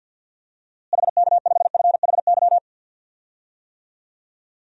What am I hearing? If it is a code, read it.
Morse code, heard as a keyed tone, about 52 wpm, 700 Hz. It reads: HK5FHY
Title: DXCC-100 Call Signs - 6 Characters - Repeated 2x - 50wpm